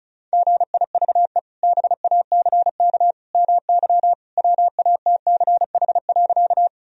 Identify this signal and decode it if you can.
Morse code, heard as a keyed tone, about 36 wpm, 700 Hz. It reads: GIVE BACK MY WATCH.
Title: Sentences from Top 1000 Words - Review Part 1 - 35wpm